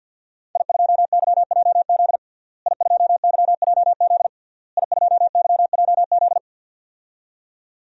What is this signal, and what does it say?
Morse code, heard as a keyed tone, about 52 wpm, 700 Hz. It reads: I2YJZ I2YJZ I2YJZ
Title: DXCC-100 Call Signs - 5 Characters - Repeated 3x - 50wpm